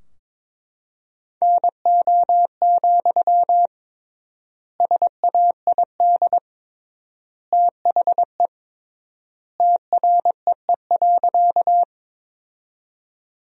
Morse code, heard as a keyed tone, about 21 wpm, 700 Hz. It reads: NO, SAID THE TREE.
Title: 3x Word-Spacing - Sentences from Top 300 Words - 22wpm